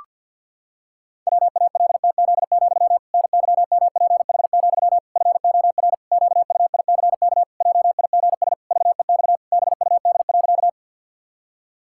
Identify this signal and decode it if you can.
Morse code, heard as a keyed tone, about 52 wpm, 700 Hz. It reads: WALTZ, NYMPH, FOR QUICK JIGS VEX BUD.
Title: Pangram Challenge v2 - 50wpm